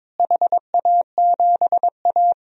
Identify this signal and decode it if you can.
Morse code, heard as a keyed tone, about 21 wpm, 700 Hz. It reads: HA7A
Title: DXCC-100 Call Signs - 4 Characters - Repeated 2x - 22wpm